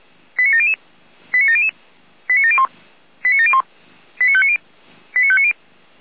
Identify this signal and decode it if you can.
Five-tone selective call (ZVEI), heard as quick runs of five tones, tones of about 70 ms. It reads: 89700 89700 89701 89701 89590 89590
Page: audio.f85360.feuerwehr-orlamuende.org